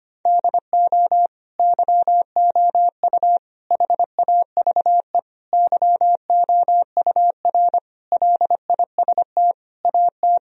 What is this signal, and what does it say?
Morse code, heard as a keyed tone, about 25 wpm, 700 Hz. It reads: DO YOU HAVE YOUR LIST AT
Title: Sentences from Top 1000 Words - Review Part 2 - 25wpm